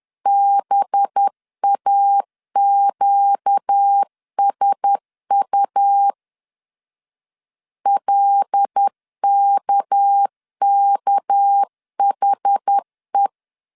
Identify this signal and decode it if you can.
Morse code, heard as a keyed tone, about 10 wpm, 800 Hz. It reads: BAQSU LKKHE